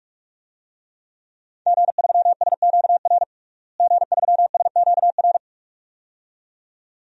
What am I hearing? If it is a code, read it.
Morse code, heard as a keyed tone, about 46 wpm, 700 Hz. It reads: G3SQR G3SQR
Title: DXCC-100 Call Signs - 5 Characters - Repeated 2x - 45wpm